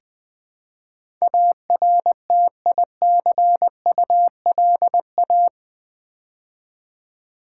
Morse code, heard as a keyed tone, about 20 wpm, 700 Hz. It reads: ARTICULA
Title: Build A Word - 4 Letters At A Time - 10-Letter Words - 20wpm